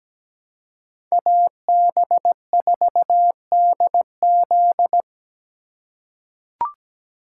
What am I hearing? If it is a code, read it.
Morse code, heard as a keyed tone, about 17 wpm, 700 Hz. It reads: AB4DZ
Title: USA Call Signs - 3x Character Spacing - 17wpm